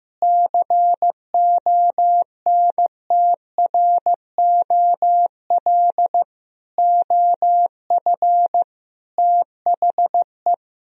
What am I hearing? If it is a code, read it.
Morse code, heard as a keyed tone, about 15 wpm, 700 Hz. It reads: CONTROL OF THE
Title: Sentences from Top 1000 Words - Review Part 3 - 15wpm